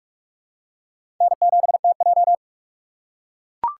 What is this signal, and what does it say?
Morse code, heard as a keyed tone, about 44 wpm, 700 Hz. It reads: N7TJ
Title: USA Call Signs - 4 Characters - 2x Character Spacing - 45wpm